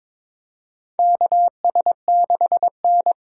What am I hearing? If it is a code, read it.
Morse code, heard as a keyed tone, about 22 wpm, 700 Hz. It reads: KS6N
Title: USA Call Signs - 4 Characters - 4x Character Spacing - 22wpm